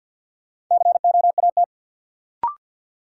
Morse code, heard as a keyed tone, about 48 wpm, 700 Hz. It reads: COAT
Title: Sentences from Top 1600 Words - New Word Drill Alt - 50wpm